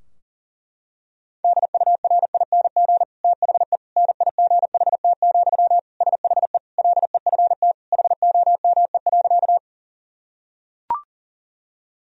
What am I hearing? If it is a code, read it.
Morse code, heard as a keyed tone, about 40 wpm, 700 Hz. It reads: DURING THE NIGHT, SHE LEFT HOME.
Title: Sentences from Top 1000 Words - Review Part 2 - 40wpm